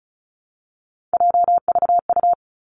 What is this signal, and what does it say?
Morse code, heard as a keyed tone, about 36 wpm, 700 Hz. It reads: JVU
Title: Three Challenging Letter-Number Combinations 35wpm